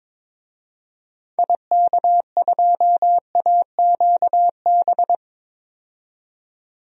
Morse code, heard as a keyed tone, about 22 wpm, 700 Hz. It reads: IK2AQB
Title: DXCC-100 Call Signs - 6 Characters - 22wpm